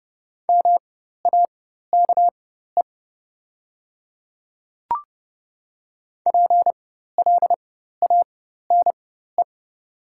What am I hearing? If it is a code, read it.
Morse code, heard as a keyed tone, about 31 wpm, spaced as for 14 wpm, 700 Hz. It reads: MAKE PLANE
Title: 4x Character Spacing - Top 500 Words - 30wpm